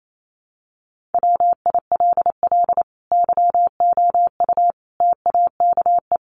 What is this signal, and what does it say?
Morse code, heard as a keyed tone, about 28 wpm, 700 Hz. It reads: WILL YOU TAKE
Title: Sentences from Top 1100 Words 28wpm